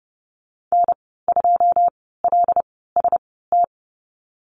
Morse code, heard as a keyed tone, about 30 wpm, spaced as for 17 wpm, 700 Hz. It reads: N2LST
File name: Lesson 14g - Number 2 - Call Signs 30wpm